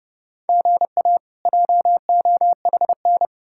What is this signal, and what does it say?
Morse code, heard as a keyed tone, about 39 wpm, 700 Hz. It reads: GA JOHN